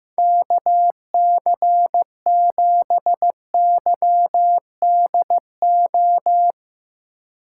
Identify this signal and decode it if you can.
Morse code, heard as a keyed tone, about 17 wpm, 700 Hz. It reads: KC7YDO
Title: Speed Racing - USA 2x3 Call Signs 15wpm